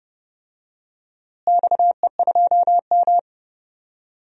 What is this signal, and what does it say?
Morse code, heard as a keyed tone, about 30 wpm, 700 Hz. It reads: XE2M
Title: DXCC-100 Call Signs - 4 Characters - Repeated 3x - 30wpm